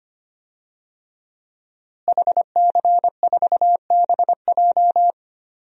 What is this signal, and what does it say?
Morse code, heard as a keyed tone, about 25 wpm, 700 Hz. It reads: HC4BJ
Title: DXCC-100 Call Signs - 5 Characters - Repeated 2x - 25wpm